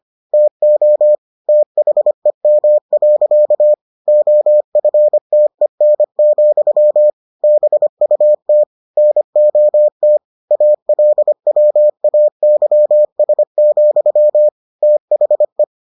Morse code, heard as a keyed tone, about 25 wpm, 600 Hz. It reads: TO THEM. OFTEN, BUT NOT ALWAYS, THE